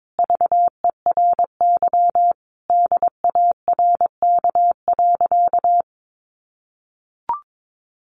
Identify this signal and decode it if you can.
Morse code, heard as a keyed tone, about 22 wpm, 700 Hz. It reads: VERY DARK.
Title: Sentences from Top 500 Words 22wpm